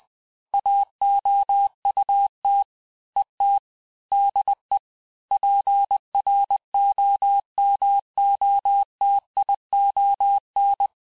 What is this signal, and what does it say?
Morse code, heard as a keyed tone, about 20 wpm, 800 Hz. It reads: AOUT ET DE PROMOTION
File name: cours-du-mercredi / audios / cours-858-texte-clais-vitesse-19mpm